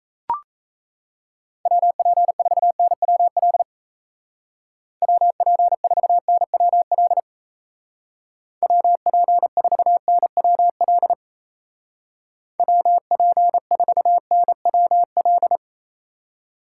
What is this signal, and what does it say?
Morse code, heard as a keyed tone, about 35 wpm, 700 Hz. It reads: WP4NWL WP4NWL WP4NWL WP4NWL